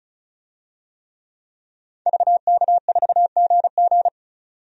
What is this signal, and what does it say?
Morse code, heard as a keyed tone, about 36 wpm, 700 Hz. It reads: VK4GG
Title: DXCC-100 Call Signs - 5 Characters - 35wpm